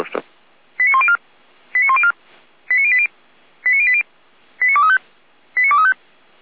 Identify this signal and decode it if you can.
Five-tone selective call (ZVEI), heard as quick runs of five tones, tones of about 70 ms, with speech. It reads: 89104 89104 89080 89080 89235 89235